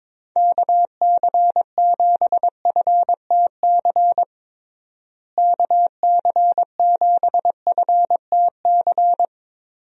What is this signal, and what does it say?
Morse code, heard as a keyed tone, about 21 wpm, 700 Hz. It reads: KC7FTC KC7FTC
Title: USA Call Signs - 6 Characters - Repeated 3x - 22wpm